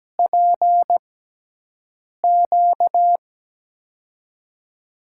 Morse code, heard as a keyed tone, about 17 wpm, spaced as for 6 wpm, 700 Hz. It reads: PQ
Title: USA Call Signs - 6x Character Spacing - 17wpm